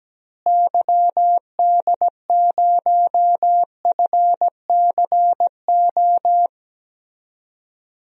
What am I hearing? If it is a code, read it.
Morse code, heard as a keyed tone, about 17 wpm, 700 Hz. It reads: YD0FCO
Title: DXCC-100 Call Signs - 6 Characters - Repeated 2x - 17wpm